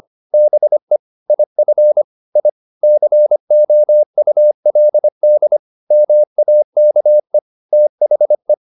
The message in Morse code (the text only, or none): BE IF I COULD MAKE THE